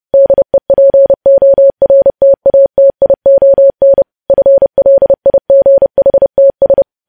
Morse code, heard as a keyed tone, about 30 wpm, 550 Hz. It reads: DEPORTATION FLIGHTS
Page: news headlines in morse code at 30 wpm